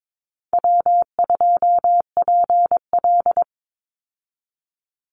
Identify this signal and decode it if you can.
Morse code, heard as a keyed tone, about 22 wpm, 700 Hz. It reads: W2PL
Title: Easy Call Signs - 22wpm